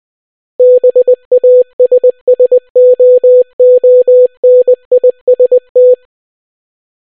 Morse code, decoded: BASSOONIST